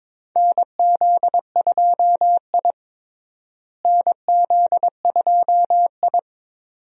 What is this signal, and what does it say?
Morse code, heard as a keyed tone, about 22 wpm, 700 Hz. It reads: NZ2I NZ2I
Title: USA Call Signs - 4 Characters - Repeated 2x - 22wpm